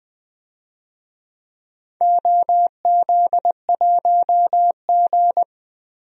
Morse code, heard as a keyed tone, about 20 wpm, 700 Hz. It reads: OZ1G
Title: DXCC-100 Call Signs - 4 Characters - 20wpm